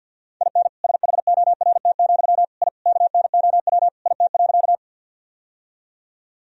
Morse code, heard as a keyed tone, about 52 wpm, 700 Hz. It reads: IN SHORT, I KNOW IT.
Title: Sentences from Top 1000 Words - Review Part 2 - 50wpm